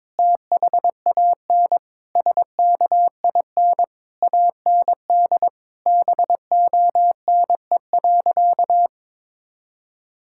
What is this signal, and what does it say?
Morse code, heard as a keyed tone, about 21 wpm, 700 Hz. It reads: THAN SKIN AND BONE.
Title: Sentences from Top 1600 Words 22wpm